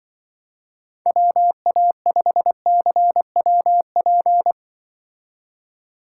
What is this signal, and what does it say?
Morse code, heard as a keyed tone, about 24 wpm, 700 Hz. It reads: WA5CWP
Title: Speed Racing - USA Call Signs 20wpm